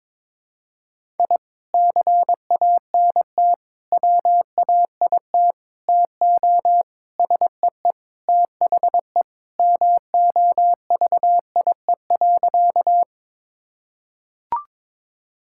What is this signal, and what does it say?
Morse code, heard as a keyed tone, about 22 wpm, 700 Hz. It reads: I CANT WAIT TO SEE THE MOVIE.
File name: practice / 22wpm / Sentences with 3-Word Phrases - 22wpm